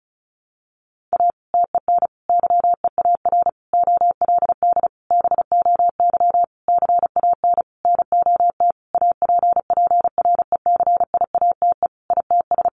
Morse code, heard as a keyed tone, about 34 wpm, 700 Hz. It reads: A TEN YEAR OLD BOY CAN NOT APPRECIATE ITS
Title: Sentences from Top 1300 Words 35wpm